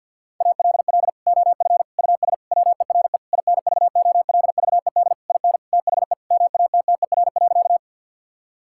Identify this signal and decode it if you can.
Morse code, heard as a keyed tone, about 50 wpm, 700 Hz. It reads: ALL OF US WERE INVOLVED IN THE MATTER.
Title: Sentences from Top 1000 Words - Review Part 5 - 50wpm